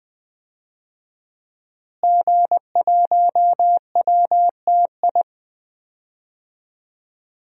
Morse code, heard as a keyed tone, about 20 wpm, 700 Hz. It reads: G1WTI
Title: DXCC-100 Call Signs - 5 Characters - Repeated 2x - 20wpm